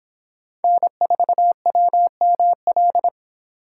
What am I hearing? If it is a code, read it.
Morse code, heard as a keyed tone, about 25 wpm, 700 Hz. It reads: N4WML